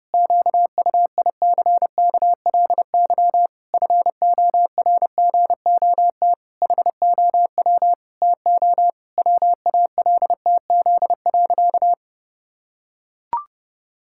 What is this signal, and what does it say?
Morse code, heard as a keyed tone, about 32 wpm, 700 Hz. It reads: QUICKLY FORGOT HOW TO WALTZ.
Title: Pangram Challenge v2 - 30wpm